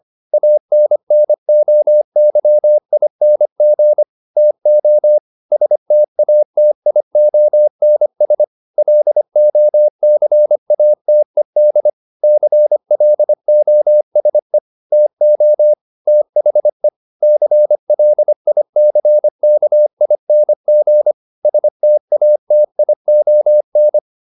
Morse code, heard as a keyed tone, about 25 wpm, 600 Hz. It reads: ANNOYING TO STATIONS LOCATED CLOSE TO THE CLICKING STATION